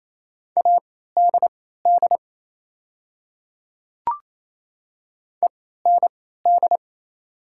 Morse code, heard as a keyed tone, about 27 wpm, spaced as for 16 wpm, 690 Hz. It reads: ADD END
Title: 3x Character Spacing - Top 200 Words - 28wpm